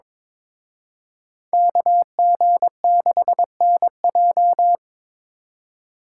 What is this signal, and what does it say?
Morse code, heard as a keyed tone, about 26 wpm, 700 Hz. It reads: KG6NJ
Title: Speed Racing - USA 2x2 Call Signs 22wpm